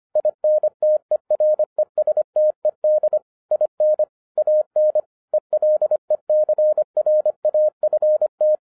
Morse code, heard as a keyed tone, about 25 wpm, 610 Hz. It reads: INTERESTED IN AN ELECRAFT
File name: 2011-07-mp3 / n24 QRP OTAKU 25wpm